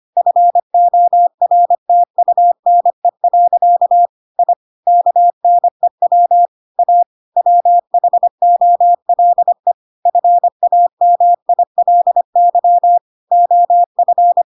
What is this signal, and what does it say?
Morse code, heard as a keyed tone, about 25 wpm, 700 Hz. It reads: FORTUNE. I KNEW A WHOLE FAMILY OF
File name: Pinocchio / Pinocchio ch3 25 25 ik1hge